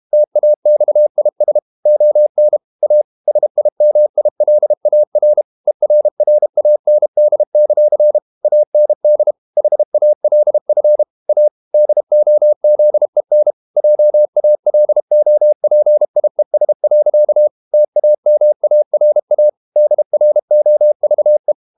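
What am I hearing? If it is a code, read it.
Morse code, heard as a keyed tone, about 32 wpm, 600 Hz. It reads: TAXIS ON A SIMILAR ERRAND; AND HALF A DOZEN JALOPIES. TAMARA DROVE
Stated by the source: Morse code practice at 32 wpm from CW med Gnister, reading Ken Follett's novel Never in English.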